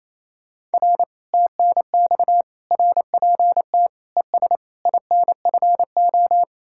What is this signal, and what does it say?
Morse code, heard as a keyed tone, about 36 wpm, 700 Hz. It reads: R TNX RPT ES INFO